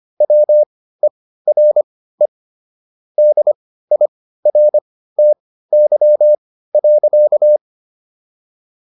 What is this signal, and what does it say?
Morse code, heard as a keyed tone, about 25 wpm, spaced as for 15 wpm, 600 Hz. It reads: WERE DIRTY.